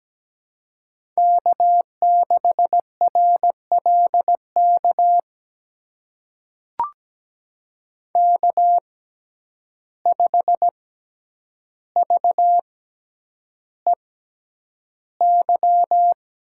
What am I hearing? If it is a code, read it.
Morse code, heard as a keyed tone, about 17 wpm, spaced as for 9 wpm, 700 Hz. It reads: K6RLK K5VEY
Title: USA Call Signs - 5 Characters - 6x Character Spacing - 17wpm